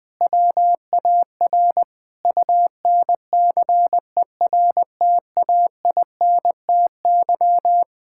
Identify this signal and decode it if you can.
Morse code, heard as a keyed tone, about 20 wpm, 700 Hz. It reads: WAR UNCERTAINTY
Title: Sentences with 6-Word Phrases - 20wpm